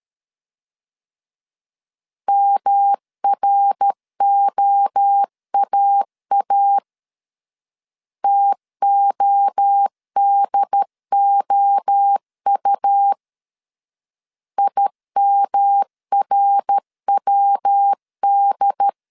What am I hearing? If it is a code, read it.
Morse code, heard as a keyed tone, about 13 wpm, 800 Hz. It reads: MROAA TODOU IMRWD
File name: CW / DET / 1 3 1500C S C 12wpm 2750